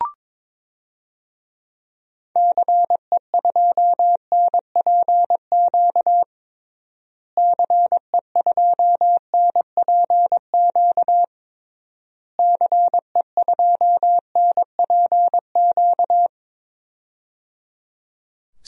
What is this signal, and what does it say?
Morse code, heard as a keyed tone, about 22 wpm, 700 Hz. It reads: CE2NPQ CE2NPQ CE2NPQ
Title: DXCC-100 Call Signs - 6 Characters - Repeated 3x - 22wpm